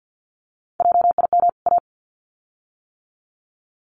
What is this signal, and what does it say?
Morse code, heard as a keyed tone, about 50 wpm, 700 Hz. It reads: JIM A
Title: CWT Exchange 50wpm